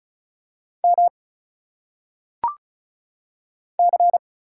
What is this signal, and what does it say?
Morse code, heard as a keyed tone, about 34 wpm, 700 Hz. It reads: M C
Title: Single Letters - Phonetic Pronunciation 35wpm